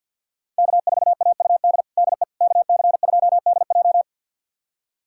Morse code, heard as a keyed tone, about 52 wpm, 700 Hz. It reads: K4AUD DE KC2DJ
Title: USA Callsign DE Callsign - 5 Characters - 50wpm